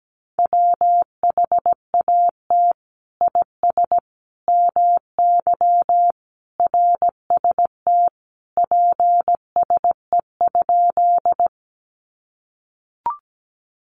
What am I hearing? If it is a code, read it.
Morse code, heard as a keyed tone, about 17 wpm, 700 Hz. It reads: WHAT IS MY RST PSE?
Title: Common QSO Phrases 17wpm